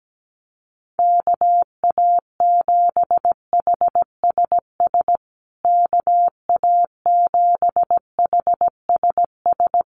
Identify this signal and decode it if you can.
Morse code, heard as a keyed tone, about 17 wpm, 700 Hz. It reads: KA7HSS KA7HSS
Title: USA Call Signs Repeated 2x 17wpm